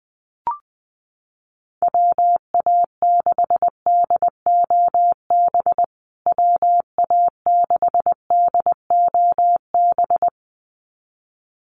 Morse code, heard as a keyed tone, about 20 wpm, 700 Hz. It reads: WA6DOB WA6DOB
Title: USA Call Signs Repeated 2x 20wpm